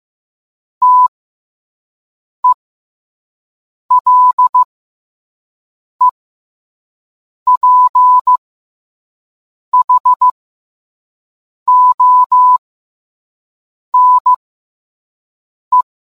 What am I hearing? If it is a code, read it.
Morse code, heard as a keyed tone, about 15 wpm, spaced as for 5 wpm, 1000 Hz. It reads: TELEPHONE